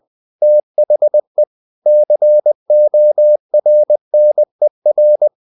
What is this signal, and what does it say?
Morse code, heard as a keyed tone, about 20 wpm, 600 Hz. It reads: THE CORNER
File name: the old man and the sea / the old man and the sea part 1 20wpm